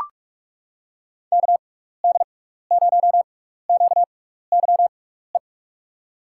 Morse code, heard as a keyed tone, about 44 wpm, spaced as for 15 wpm, 700 Hz. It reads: KD0QYE
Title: USA Call Signs - 6 Characters - 6x Character Spacing - 45wpm